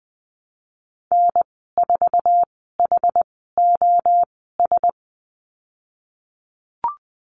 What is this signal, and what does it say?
Morse code, heard as a keyed tone, about 20 wpm, spaced as for 14 wpm, 700 Hz. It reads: N4HOS
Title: 076 Lesson 13g - Letter L - Call Signs 20wpm